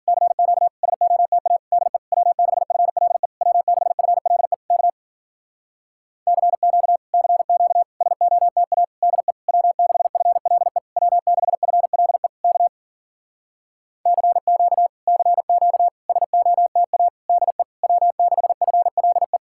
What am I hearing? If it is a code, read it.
Morse code, heard as a keyed tone, about 52 wpm, 700 Hz. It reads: CQ SOTA DE W6FLE W6FLE K CQ CQ SOTA DE W6FLE W6FLE K CQ CQ SOTA DE W6FLE